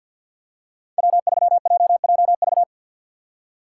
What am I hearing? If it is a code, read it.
Morse code, heard as a keyed tone, about 51 wpm, 700 Hz. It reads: W3JJV
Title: USA Call Signs - 5 Characters - 2x Character Spacing - 50wpm